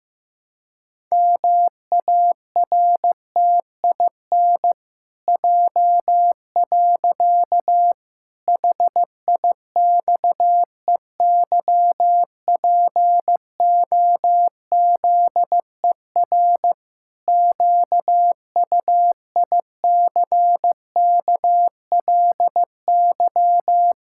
Morse code, heard as a keyed tone, about 15 wpm, 700 Hz. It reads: MARTIN J. HIXEYPOZER QUICKLY
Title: Pangram Challenge v2 - 15wpm